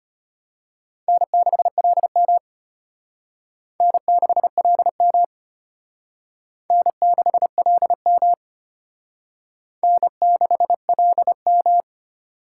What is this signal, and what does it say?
Morse code, heard as a keyed tone, about 32 wpm, 700 Hz. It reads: N6LM N6LM N6LM N6LM